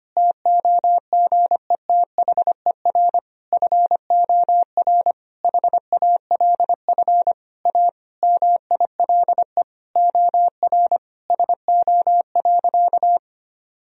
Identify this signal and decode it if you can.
Morse code, heard as a keyed tone, about 25 wpm, 700 Hz. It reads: TOGETHER FOR HALF A MILE OR SO.
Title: Sentences from Top 1000 Words - Review Part 1 - 25wpm